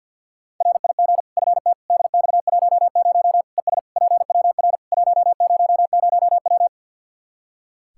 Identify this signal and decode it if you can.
Morse code, heard as a keyed tone, about 46 wpm, 700 Hz. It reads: RIG FT DX10 ES PWR 100W
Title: Practice QSO - Exchange 2 - Info - 50wpm